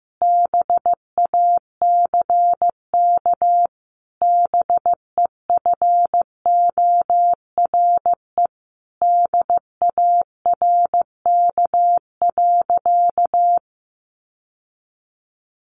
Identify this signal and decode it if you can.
Morse code, heard as a keyed tone, about 15 wpm, 700 Hz. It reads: BACK BEFORE DARK.
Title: Sentences from Top 1100 Words 15wpm